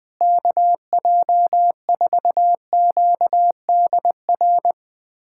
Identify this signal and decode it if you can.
Morse code, heard as a keyed tone, about 21 wpm, 700 Hz. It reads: KJ4QDR
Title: USA Callsign DE Callsign - 6 Characters - 20wpm